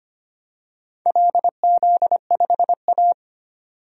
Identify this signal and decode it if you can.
Morse code, heard as a keyed tone, about 25 wpm, 700 Hz. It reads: LZ5A
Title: DXCC-100 Call Signs - 4 Characters - 25wpm